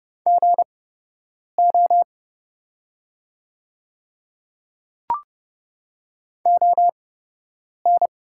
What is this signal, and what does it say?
Morse code, heard as a keyed tone, about 31 wpm, spaced as for 8 wpm, 700 Hz. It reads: GO ON
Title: 8x Character Spacing - Top 100 Words - 30wpm